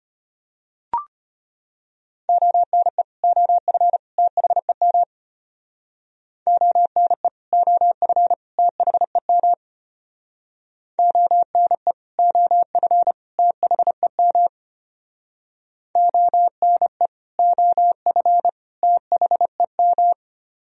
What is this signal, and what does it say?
Morse code, heard as a keyed tone, about 32 wpm, 700 Hz. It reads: ONE OF THEM ONE OF THEM ONE OF THEM ONE OF THEM